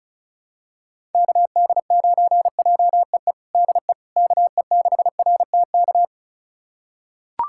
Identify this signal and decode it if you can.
Morse code, heard as a keyed tone, about 36 wpm, 700 Hz. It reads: KD9JEE DE KE6RTK
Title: USA Callsign DE Callsign - 6 Characters - 35wpm